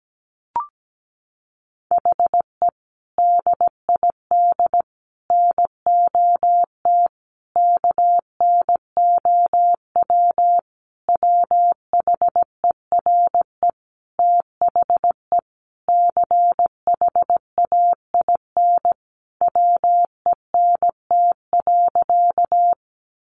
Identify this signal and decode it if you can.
Morse code, heard as a keyed tone, about 17 wpm, 700 Hz. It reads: HE DID NOT KNOW WHERE THE CHAIN WENT.
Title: Sentences from Top 1500 Words 17wpm